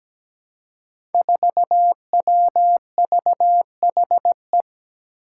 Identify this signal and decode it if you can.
Morse code, heard as a keyed tone, about 17 wpm, 700 Hz. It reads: 4WVHE
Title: Five Challenging Letter-Number Combinations 17wpm